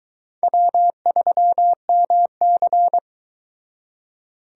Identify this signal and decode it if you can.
Morse code, heard as a keyed tone, about 22 wpm, 700 Hz. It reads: W3MC